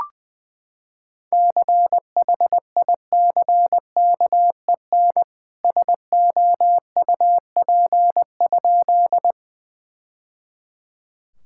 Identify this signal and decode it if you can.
Morse code, heard as a keyed tone, about 20 wpm, 700 Hz. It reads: CHICKEN SOUP?
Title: Sentences from Top 1000 Words - Review Part 6 - 20wpm